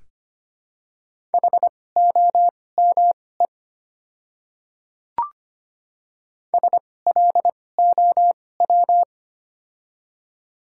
Morse code, heard as a keyed tone, about 25 wpm, spaced as for 18 wpm, 700 Hz. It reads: HOME SLOW
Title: 2x Character Spacing - Top 500 Words - 25wpm